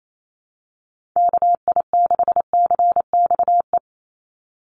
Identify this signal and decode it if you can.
Morse code, heard as a keyed tone, about 28 wpm, 700 Hz. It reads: KI6CXE
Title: USA Call Signs 28wpm